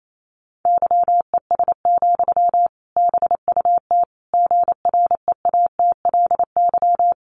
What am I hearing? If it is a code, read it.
Morse code, heard as a keyed tone, about 28 wpm, 700 Hz. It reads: YES, BUT GREATLY
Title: Sentences from Top 1400 Words 28wpm